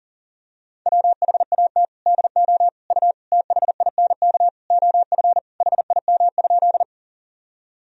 Morse code, heard as a keyed tone, about 40 wpm, 700 Hz. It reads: WHAT DO U THINK OF HIM?
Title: Sentences from Top 200 Words and Abbreviations 40wpm